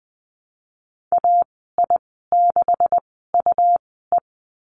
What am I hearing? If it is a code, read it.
Morse code, heard as a keyed tone, about 20 wpm, spaced as for 15 wpm, 700 Hz. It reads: AI6UE
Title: 186 Lesson 30g - Letter b - Call Signs 20wpm